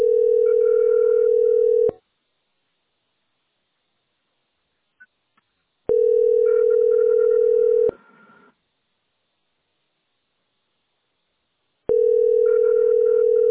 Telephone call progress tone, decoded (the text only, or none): ringback tone